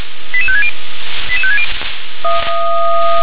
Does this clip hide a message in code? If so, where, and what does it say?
Yes: five-tone selective call (ZVEI) from 0.3 to 1.6 s, heard as quick runs of five tones, tones of about 70 ms. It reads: 88560 88560